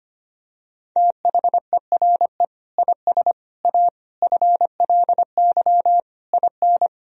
Morse code, heard as a keyed tone, about 25 wpm, 700 Hz. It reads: THERE IS A FLY IN
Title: Sentences from Top 1000 Words - Review Part 4 - 25wpm